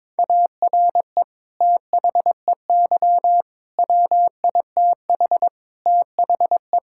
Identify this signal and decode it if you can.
Morse code, heard as a keyed tone, about 22 wpm, 700 Hz. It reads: ARE THEY WITH THE
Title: Sentences from Top 1000 Words - Review Part 5 - 22wpm